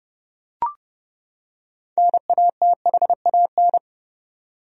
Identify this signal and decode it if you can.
Morse code, heard as a keyed tone, about 32 wpm, 700 Hz. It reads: NATHAN
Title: USA Common Names - 30wpm